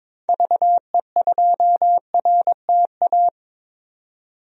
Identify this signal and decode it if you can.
Morse code, heard as a keyed tone, about 21 wpm, 700 Hz. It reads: VE2RTA
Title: DXCC-100 Call Signs - 6 Characters - 22wpm